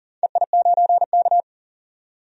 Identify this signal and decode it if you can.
Morse code, heard as a keyed tone, about 40 wpm, 700 Hz. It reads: EI9K
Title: DXCC-100 Call Signs - 4 Characters - 40wpm